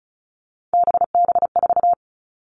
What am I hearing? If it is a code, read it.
Morse code, heard as a keyed tone, about 34 wpm, 700 Hz. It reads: BB4
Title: All Cut Numbers - 3-Digit 35wpm